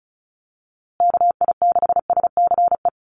Morse code, heard as a keyed tone, about 34 wpm, 700 Hz. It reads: KI6SCE
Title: USA Call Signs Repeated 2x 35wpm